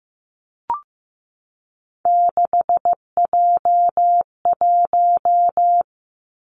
Morse code, heard as a keyed tone, about 15 wpm, 700 Hz. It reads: 6J1
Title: Three Challenging Letter-Number Combinations 15wpm